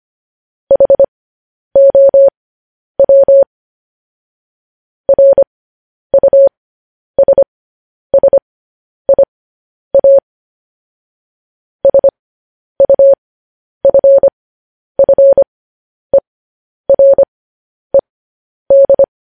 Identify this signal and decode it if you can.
Morse code, heard as a keyed tone, about 25 wpm, spaced as for 10 wpm, 550 Hz. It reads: HOW RUSSIA SUFFERED
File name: podcast / News headlines-1x-25x10